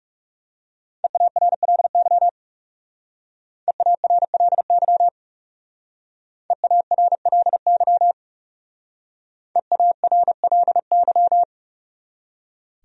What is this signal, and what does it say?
Morse code, heard as a keyed tone, about 37 wpm, 700 Hz. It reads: EARLY EARLY EARLY EARLY
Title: Speed-Racing Top 500 Words 30wpm